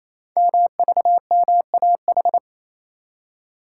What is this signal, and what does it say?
Morse code, heard as a keyed tone, about 28 wpm, 700 Hz. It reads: MVMAH
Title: Code Groups - 5 Letters - 28wpm